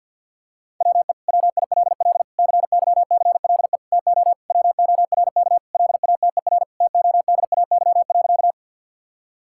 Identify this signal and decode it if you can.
Morse code, heard as a keyed tone, about 52 wpm, 700 Hz. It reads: WE WILL CYCLE TO WORK LATER TODAY.
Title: Sentences from Top 1000 Words - Review Part 6 - 50wpm